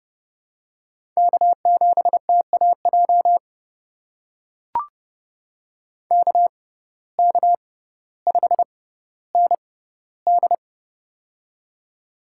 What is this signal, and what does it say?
Morse code, heard as a keyed tone, about 31 wpm, spaced as for 16 wpm, 700 Hz. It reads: K7TAJ KK5ND
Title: USA Call Signs - 5 Characters - 6x Character Spacing - 30wpm